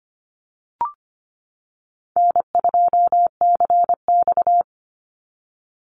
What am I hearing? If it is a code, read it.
Morse code, heard as a keyed tone, about 25 wpm, 700 Hz. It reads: N2CX
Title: Call Signs - 25wpm